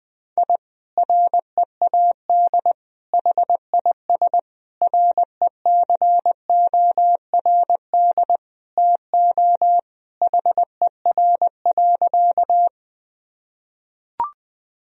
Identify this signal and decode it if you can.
Morse code, heard as a keyed tone, about 20 wpm, 700 Hz. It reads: I READ HIS RECORD TO HER.
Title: Sentences from Top 1000 Words - Review Part 3 - 20wpm